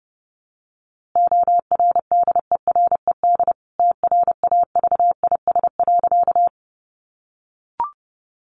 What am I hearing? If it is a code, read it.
Morse code, heard as a keyed tone, about 30 wpm, 700 Hz. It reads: ORDERED TRAVIS.
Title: A Sound Of Thunder 30wpm